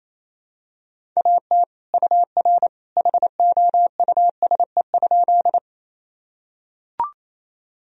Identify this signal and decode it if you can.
Morse code, heard as a keyed tone, about 28 wpm, 700 Hz. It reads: AT UR HOUSE?
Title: Sentences from Top 200 Words and Abbreviations 28wpm